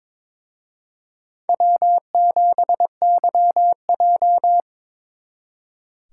Morse code, heard as a keyed tone, about 26 wpm, 700 Hz. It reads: W7YJ
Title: Speed Racing - USA 1x2 Call Signs 22wpm